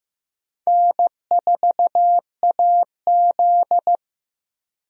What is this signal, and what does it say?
Morse code, heard as a keyed tone, about 15 wpm, 700 Hz. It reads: N4AZ
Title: USA Call Signs - 4 Characters - Repeated 2x - 15wpm